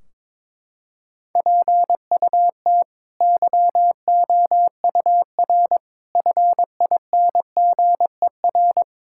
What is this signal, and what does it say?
Morse code, heard as a keyed tone, about 22 wpm, 700 Hz. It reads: PUT YOUR FINGER
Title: Sentences from Top 1600 Words 22wpm